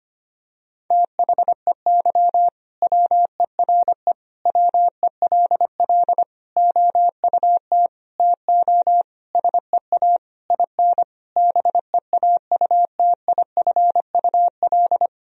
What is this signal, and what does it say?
Morse code, heard as a keyed tone, about 25 wpm, 700 Hz. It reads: THEY WERE WELL OUT TO SEA IN BEAUTIFUL
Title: Sentences from Top 1000 Words - Review Part 3 - 25wpm